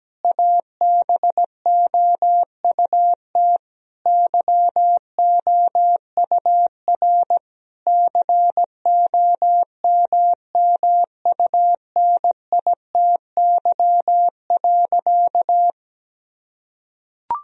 Morse code, heard as a keyed tone, about 17 wpm, 700 Hz. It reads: ABOUT YOUR COMMUNITY.
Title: Sentences from Top 1000 Words - Review Part 4 - 17wpm